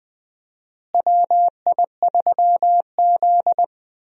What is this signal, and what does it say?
Morse code, heard as a keyed tone, about 20 wpm, 700 Hz. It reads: WI3Z
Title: USA Call Signs - 4 Characters - Repeated 3x - 20wpm